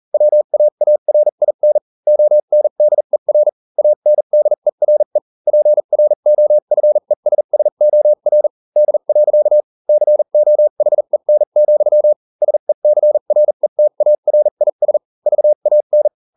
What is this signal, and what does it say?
Morse code, heard as a keyed tone, about 40 wpm, spaced as for 35 wpm, 600 Hz. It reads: WAARIN ONDER ANDERE PROFESSOR D. COHEN, SECRETARIS VAN